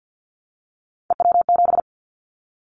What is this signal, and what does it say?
Morse code, heard as a keyed tone, about 50 wpm, 700 Hz. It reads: EW7
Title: Callsign Prefixes for Top 50 DXCC - 50wpm